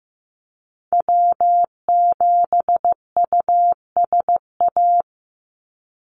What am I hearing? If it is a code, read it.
Morse code, heard as a keyed tone, about 15 wpm, 700 Hz. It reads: W7USA
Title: NA SOTA Activator Callsigns 15wpm